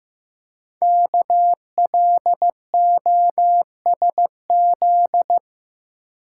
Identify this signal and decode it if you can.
Morse code, heard as a keyed tone, about 15 wpm, 700 Hz. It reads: KLOSZ
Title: Code Groups - 5 Letters - 15wpm